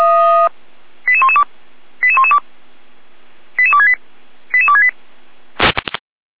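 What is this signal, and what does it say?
Five-tone selective call (ZVEI), heard as quick runs of five tones, tones of about 70 ms. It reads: 88112 88112 88268 88268
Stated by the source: ZVEI five-tone selective call received on an SDR, recorded by the Feuerwehr Orlamünde fire brigade.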